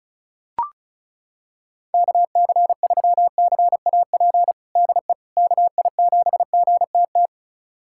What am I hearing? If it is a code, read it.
Morse code, heard as a keyed tone, about 34 wpm, 700 Hz. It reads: KC3CAP DE KI7GTT
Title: USA Callsign DE Callsign 35wpm